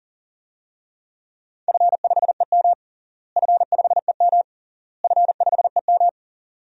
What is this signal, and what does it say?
Morse code, heard as a keyed tone, about 37 wpm, 700 Hz. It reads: F5EM F5EM F5EM
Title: DXCC-100 Call Signs - 4 Characters - Repeated 3x - 40wpm